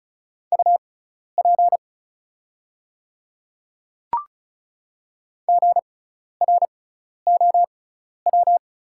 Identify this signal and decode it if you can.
Morse code, heard as a keyed tone, about 36 wpm, spaced as for 12 wpm, 690 Hz. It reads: UP GROW
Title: 6x Character Spacing - Top 500 Words - 35wpm